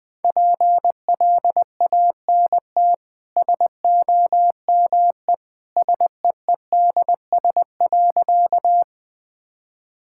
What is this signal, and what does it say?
Morse code, heard as a keyed tone, about 20 wpm, 700 Hz. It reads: PLANT SOME SEEDS.
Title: Easy Sentences with Hints - 20wpm